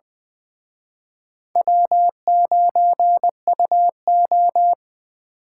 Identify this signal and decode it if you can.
Morse code, heard as a keyed tone, about 23 wpm, 700 Hz. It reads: W9UO
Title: Speed Racing - USA Call Signs 20wpm